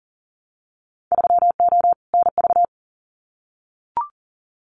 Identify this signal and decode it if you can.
Morse code, heard as a keyed tone, about 40 wpm, 700 Hz. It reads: 3O NV
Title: Winter Field Day Exchanges Encore 40wpm